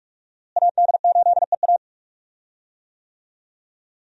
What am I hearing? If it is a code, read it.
Morse code, heard as a keyed tone, about 43 wpm, 700 Hz. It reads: AD8EA
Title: USA Call Signs - 5 Characters - Repeated 3x - 45wpm